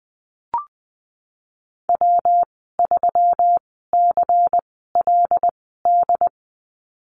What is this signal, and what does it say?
Morse code, heard as a keyed tone, about 20 wpm, spaced as for 15 wpm, 700 Hz. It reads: W3CLD
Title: 122 Lesson 20g - Number 3 - Call Signs 20wpm